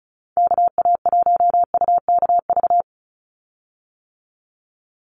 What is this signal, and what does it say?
Morse code, heard as a keyed tone, about 34 wpm, 700 Hz. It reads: KA1UKV
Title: USA Call Signs 35wpm